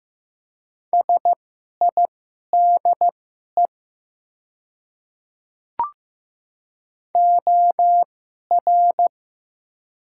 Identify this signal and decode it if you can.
Morse code, heard as a keyed tone, about 15 wpm, spaced as for 11 wpm, 690 Hz. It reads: SIDE OR
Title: 2x Character Spacing - Top 100 Words - 15wpm